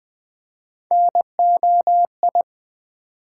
Morse code, heard as a keyed tone, about 20 wpm, 700 Hz. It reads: NOI
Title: Build A Word - 1 Letter At A Time - 5-Letter Words - 20wpm